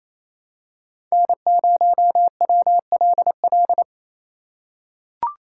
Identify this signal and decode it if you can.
Morse code, heard as a keyed tone, about 28 wpm, 700 Hz. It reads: N0WLL
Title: USA Call Signs - 5 Characters - 4x Character Spacing - 28wpm